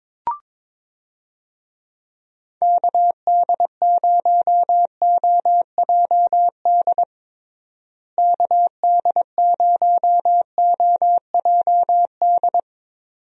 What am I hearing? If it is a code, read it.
Morse code, heard as a keyed tone, about 22 wpm, 700 Hz. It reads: KD0OJD KD0OJD
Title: USA Call Signs - 6 Characters - Repeated 2x - 22wpm